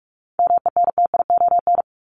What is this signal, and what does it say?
Morse code, heard as a keyed tone, about 44 wpm, 700 Hz. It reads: MENTION
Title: ED Suffix - Top 100 Words with Base Words 45wpm